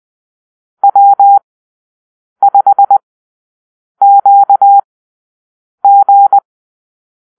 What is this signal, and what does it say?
Morse code, heard as a keyed tone, about 20 wpm, spaced as for 7 wpm, 800 Hz. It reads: W5QG
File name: arrl / Call-Sign Drill-7